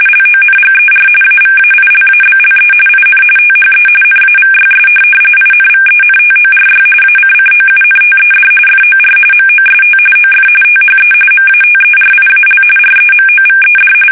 RTTY, about 45.45 baud. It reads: LONGER AVAILABLE IN THE NAVYS INVENTORY ACCOUNTED FOR MOST OF THE MATERIAL REMOVED
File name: fsk850 / USS-Massachusetts